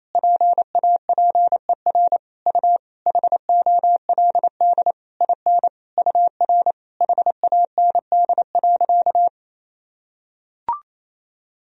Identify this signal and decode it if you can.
Morse code, heard as a keyed tone, about 28 wpm, 700 Hz. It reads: PAPER U HOLD IN UR HAND.
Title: Sentences from Top 400 Words and Abbreviations 28wpm